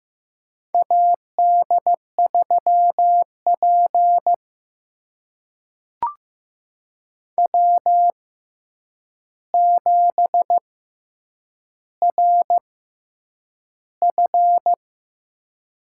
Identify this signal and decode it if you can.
Morse code, heard as a keyed tone, about 15 wpm, spaced as for 8 wpm, 700 Hz. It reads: AD3P W7RF
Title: USA Call Signs - 4 Characters - 6x Character Spacing - 15wpm